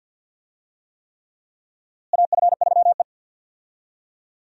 Morse code, heard as a keyed tone, about 52 wpm, 700 Hz. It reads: AF3E
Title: USA Call Signs - 4 Characters - Repeated 3x - 50wpm